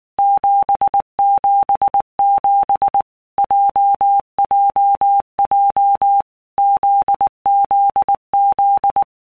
Morse code, heard as a keyed tone, about 19 wpm, 800 Hz. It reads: 777 JJJ ZZZ
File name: All Groups 1-minute Blocks / Groups@18 / JZ17